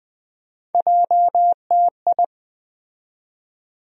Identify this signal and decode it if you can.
Morse code, heard as a keyed tone, about 20 wpm, 700 Hz. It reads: JTI
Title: USA Call Sign Postfixes - 3 Letters 20wpm